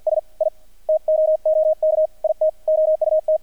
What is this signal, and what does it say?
Morse code, heard as a keyed tone, about 48 wpm, 620 Hz. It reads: S I TOOK IT OUT
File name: archived / CRSnet